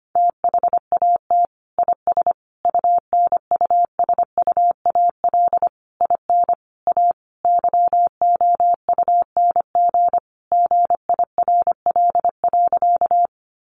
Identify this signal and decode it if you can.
Morse code, heard as a keyed tone, about 25 wpm, 700 Hz. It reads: THAT IS UNUSUAL IN A YOUNG GIRL.
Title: Sentences from Top 1400 Words 25wpm